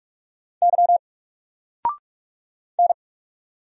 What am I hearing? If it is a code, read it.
Morse code, heard as a keyed tone, about 44 wpm, 700 Hz. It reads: Y N
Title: Single Letter-Number Alt 45wpm